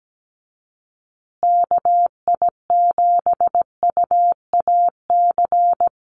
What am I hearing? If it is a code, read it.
Morse code, heard as a keyed tone, about 17 wpm, 700 Hz. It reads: KI7UAC
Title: USA Call Signs Repeated 3x 17wpm